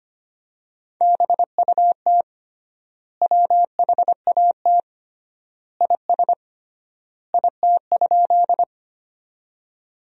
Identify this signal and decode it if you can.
Morse code, heard as a keyed tone, about 25 wpm, 700 Hz. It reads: BUT WHAT IS IT?
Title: 3x Word-Spacing - Sentences from Top 100 Words - 25wpm